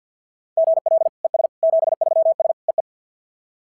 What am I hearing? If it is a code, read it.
Morse code, heard as a keyed tone, about 52 wpm, 690 Hz. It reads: GL ES 73S EE
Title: SOTA QSO Practice 50wpm